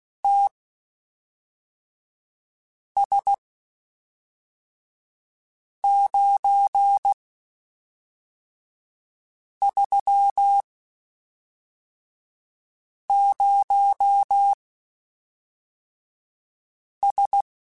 Morse code, heard as a keyed tone, about 16 wpm, spaced as for 3 wpm, 800 Hz. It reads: TS930S